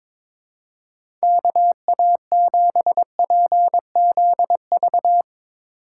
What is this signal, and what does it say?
Morse code, heard as a keyed tone, about 21 wpm, 700 Hz. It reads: KA7PZV
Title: USA Call Signs - 6 Characters - 2x Character Spacing - 22wpm